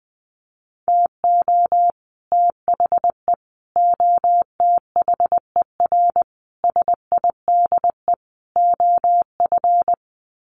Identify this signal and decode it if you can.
Morse code, heard as a keyed tone, about 20 wpm, 700 Hz. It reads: TO THE OTHER SIDE OF
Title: Sets of 5 Words Encore 20wpm